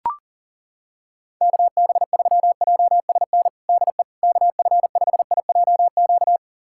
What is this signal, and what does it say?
Morse code, heard as a keyed tone, about 40 wpm, 700 Hz. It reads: KB3JSN DE KF5IJQ
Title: USA Callsign DE Callsign - 6 Characters - 40wpm